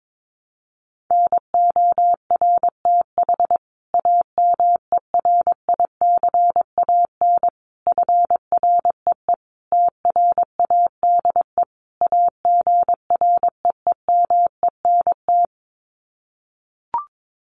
Morse code, heard as a keyed tone, about 22 wpm, 700 Hz. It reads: NORTH AMERICAN FREE TRADE AGREEMENT
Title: Sets of 5 Words Encore 22wpm